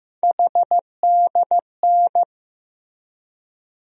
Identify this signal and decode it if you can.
Morse code, heard as a keyed tone, about 15 wpm, 700 Hz. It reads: HDN
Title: USA Call Sign Postfixes - 3 Letters 15wpm